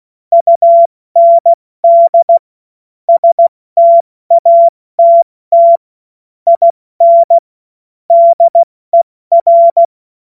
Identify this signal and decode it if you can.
Morse code, heard as a keyed tone, about 16 wpm, spaced as for 14 wpm, 680 Hz. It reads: UND STATT IN DER
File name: Morsekurs / Witze / Witz 19. Oktober 2021, 70 F80